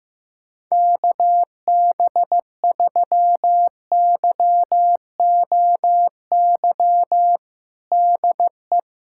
Morse code, heard as a keyed tone, about 15 wpm, 700 Hz. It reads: KB3YOY DE
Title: USA Callsign DE Callsign 15wpm